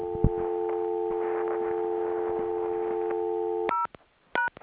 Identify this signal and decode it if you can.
Dial tone (North America), 350 + 440 Hz, with noise.